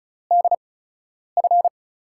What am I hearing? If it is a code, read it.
Morse code, heard as a keyed tone, about 34 wpm, spaced as for 10 wpm, 700 Hz. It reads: DF
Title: USA Call Signs - 8x Character Spacing - 35wpm